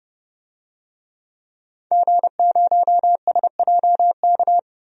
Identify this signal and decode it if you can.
Morse code, heard as a keyed tone, about 30 wpm, 700 Hz. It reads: G0SJK
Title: DXCC-100 Call Signs - 5 Characters - 30wpm